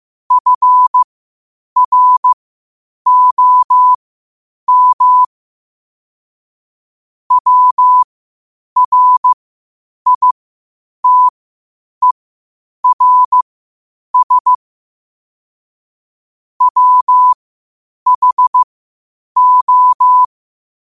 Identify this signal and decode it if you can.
Morse code, heard as a keyed tone, about 15 wpm, spaced as for 8 wpm, 1000 Hz. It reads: FROM WRITERS WHO
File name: qotdfiles / 8wpm